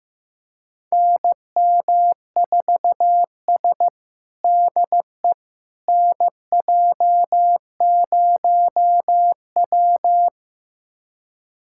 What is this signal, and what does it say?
Morse code, heard as a keyed tone, about 15 wpm, 700 Hz. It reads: NM4S DE NJ0W
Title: USA Callsign DE Callsign - 4 Characters - 15wpm